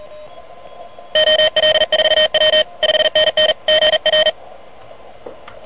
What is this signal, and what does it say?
Morse code, heard as a keyed tone, about 38 wpm, 3100 Hz, with noise. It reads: OL4W 5NN GR
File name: truSDX